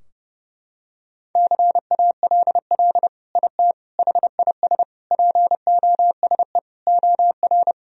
Morse code, heard as a keyed tone, about 32 wpm, 700 Hz. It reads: CALL IT HIS POSE OR
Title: Sentences from Top 1000 Words - Review Part 2 - 30wpm